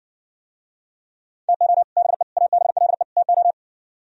Morse code, heard as a keyed tone, about 69 wpm, 700 Hz. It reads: TO BE ABLE TO